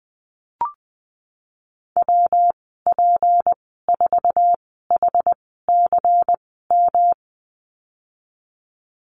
Morse code, heard as a keyed tone, about 20 wpm, spaced as for 15 wpm, 700 Hz. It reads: WP4HCM
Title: 173 Lesson 28g - Number 9 - Call Signs 20wpm